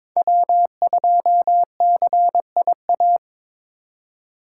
Morse code, heard as a keyed tone, about 26 wpm, 700 Hz. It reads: W2CIA
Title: Speed Racing - USA 1x3 Call Signs 22wpm